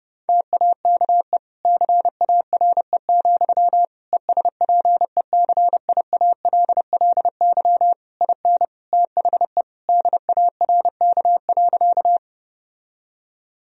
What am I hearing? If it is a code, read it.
Morse code, heard as a keyed tone, about 32 wpm, 700 Hz. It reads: TAKE CARE, ESPECIALLY IN THE DARK.
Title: Sentences from Top 1000 Words - Review Part 4 - 30wpm